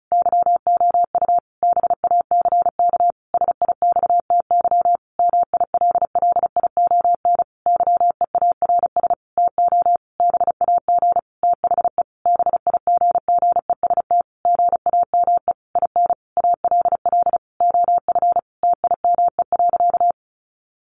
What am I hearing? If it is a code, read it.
Morse code, heard as a keyed tone, about 34 wpm, 700 Hz. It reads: YOU BACK SIXTY MILLION YEARS TO BAG THE BIGGEST GAME IN ALL OF TIME.
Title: A Sound Of Thunder 35wpm